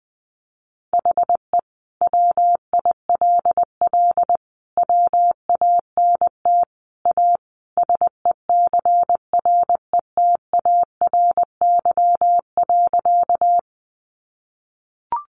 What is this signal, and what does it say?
Morse code, heard as a keyed tone, about 20 wpm, 700 Hz. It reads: HE WILL WANT A SECRETARY.
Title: Sentences from Top 1300 Words 20wpm